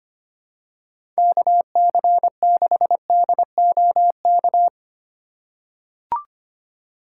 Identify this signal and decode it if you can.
Morse code, heard as a keyed tone, about 25 wpm, 700 Hz. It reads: KC6DOK
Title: USA Call Signs - 6 Characters - 2x Character Spacing - 25wpm